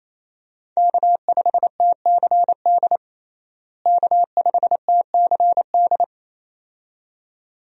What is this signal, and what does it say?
Morse code, heard as a keyed tone, about 28 wpm, 700 Hz. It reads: K5TCD K5TCD
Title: USA Call Signs - 5 Characters - Repeated 2x - 28wpm